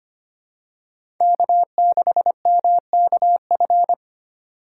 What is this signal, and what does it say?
Morse code, heard as a keyed tone, about 25 wpm, 700 Hz. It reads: K6MKF
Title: K1USN SST QSO Practice - 25wpm